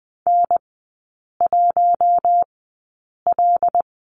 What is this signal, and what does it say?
Morse code, heard as a keyed tone, about 20 wpm, spaced as for 8 wpm, 700 Hz. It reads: N1L
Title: 205 Lesson 33e - Letter j - Three Characters 20wpm